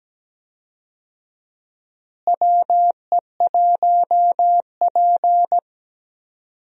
Morse code, heard as a keyed tone, about 17 wpm, 700 Hz. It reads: WE1P
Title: USA Call Signs - 4 Characters - Repeated 3x - 17wpm